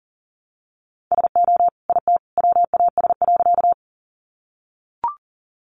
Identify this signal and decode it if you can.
Morse code, heard as a keyed tone, about 40 wpm, 700 Hz. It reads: SO IT WAS.
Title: Sentences from Top 100 Words 40wpm